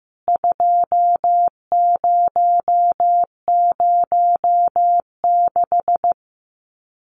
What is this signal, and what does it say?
Morse code, heard as a keyed tone, about 14 wpm, 700 Hz. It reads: 2006
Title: CWT Exchange 15wpm